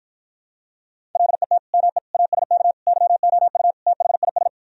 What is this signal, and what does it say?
Morse code, heard as a keyed tone, about 52 wpm, 700 Hz. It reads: LET ME ASK YOU THIS